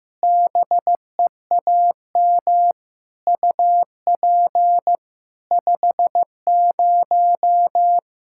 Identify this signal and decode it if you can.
Morse code, heard as a keyed tone, about 15 wpm, 700 Hz. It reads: BEAM UP 50
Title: Practice QSO - Exchange 2 - Info - 15wpm